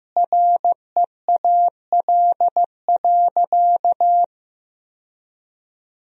Morse code, heard as a keyed tone, about 15 wpm, 700 Hz. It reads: REAL.
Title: 2x Word-Spacing - Sentences from Top 300 Words - 15wpm